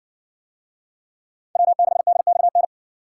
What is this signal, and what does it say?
Morse code, heard as a keyed tone, about 60 wpm, 700 Hz. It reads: W6DXN